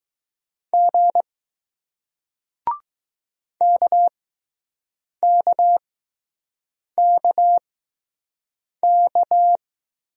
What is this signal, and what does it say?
Morse code, heard as a keyed tone, about 20 wpm, 700 Hz. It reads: G K K K K